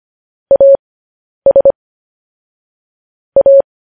Morse code, heard as a keyed tone, about 25 wpm, spaced as for 10 wpm, 550 Hz. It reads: AS A
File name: podcast / News headlines-1x-25x10